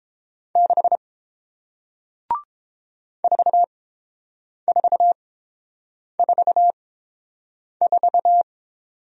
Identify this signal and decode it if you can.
Morse code, heard as a keyed tone, about 30 wpm, spaced as for 7 wpm, 700 Hz. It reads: 6 4444